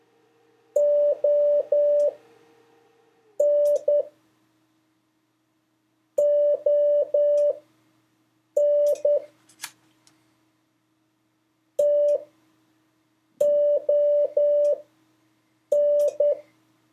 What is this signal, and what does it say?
Morse code, heard as a keyed tone, about 10 wpm, spaced as for 5 wpm, 580 Hz. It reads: ON ON TON
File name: 4of5-words-3